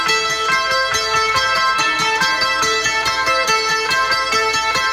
Emergency Alert System (SAME) tones, heard as harsh buzzing bursts. It is heard from 0.4 to 4.2 s.